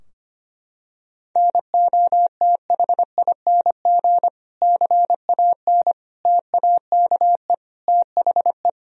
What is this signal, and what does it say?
Morse code, heard as a keyed tone, about 25 wpm, 700 Hz. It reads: NOTHING CAN TAKE THE
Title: Sentences from Top 1000 Words - Review Part 4 - 25wpm